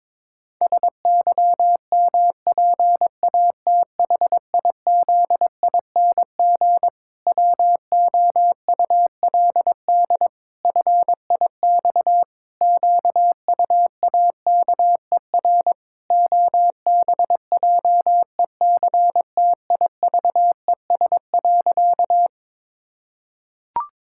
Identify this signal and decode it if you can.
Morse code, heard as a keyed tone, about 22 wpm, 700 Hz. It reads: SYMPATHIZING WOULD FIX QUAKER OBJECTIVES.
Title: Pangram Challenge v2 - 22wpm